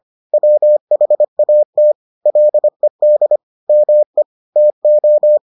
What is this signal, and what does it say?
Morse code, heard as a keyed tone, about 25 wpm, 600 Hz. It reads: WHAT LED ME TO